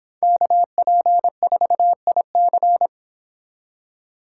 Morse code, heard as a keyed tone, about 29 wpm, 700 Hz. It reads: KP4IC